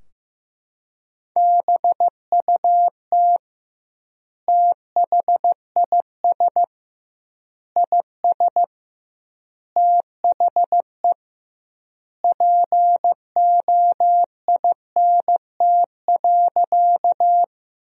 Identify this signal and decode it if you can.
Morse code, heard as a keyed tone, about 15 wpm, 700 Hz. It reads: BUT THIS IS THE POINT.
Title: 2x Word-Spacing - Sentences from Top 200 Words - 15wpm